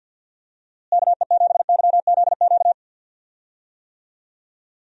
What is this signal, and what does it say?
Morse code, heard as a keyed tone, about 51 wpm, 700 Hz. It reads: KE7YZQ
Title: USA Call Signs - 6 Characters - Repeated 2x - 50wpm